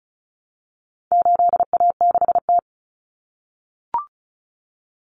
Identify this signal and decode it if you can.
Morse code, heard as a keyed tone, about 34 wpm, 700 Hz. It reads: 8A6T
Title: Common Cut Numbers - 4-Digit 35wpm